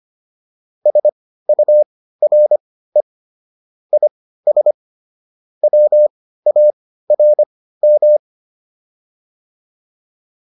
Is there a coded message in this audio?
Yes: Morse code from 0.9 to 8.2 s, heard as a keyed tone, about 25 wpm, spaced as for 15 wpm, 600 Hz. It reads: SURE IS WARM